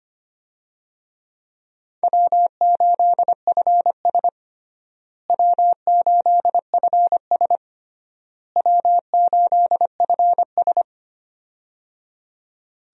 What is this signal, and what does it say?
Morse code, heard as a keyed tone, about 25 wpm, 700 Hz. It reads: W8FS W8FS W8FS
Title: USA Call Signs - 4 Characters - Repeated 3x - 25wpm